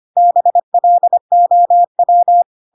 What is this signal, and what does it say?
Morse code, heard as a keyed tone, about 25 wpm, 700 Hz. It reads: BLOW
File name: Pinocchio / Pinocchio ch1 25 25 ik1hge